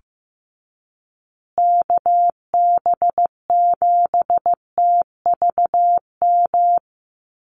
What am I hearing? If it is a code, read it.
Morse code, heard as a keyed tone, about 15 wpm, 700 Hz. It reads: KB7TVM
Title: USA Call Signs Repeated 3x 15wpm